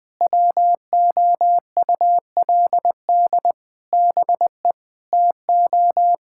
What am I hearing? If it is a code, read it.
Morse code, heard as a keyed tone, about 20 wpm, 700 Hz. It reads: WOULD BE TO
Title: Sentences with 8-Word Phrases - 20wpm